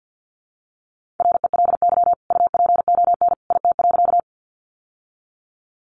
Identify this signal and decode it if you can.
Morse code, heard as a keyed tone, about 50 wpm, 700 Hz. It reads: RELY UPON IT.
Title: Sentences from Top 1200 Words 50wpm